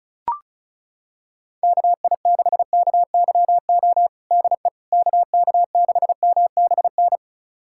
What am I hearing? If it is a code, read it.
Morse code, heard as a keyed tone, about 34 wpm, 700 Hz. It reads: KI6KYO DE KK6MBN
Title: USA Callsign DE Callsign - 6 Characters - 35wpm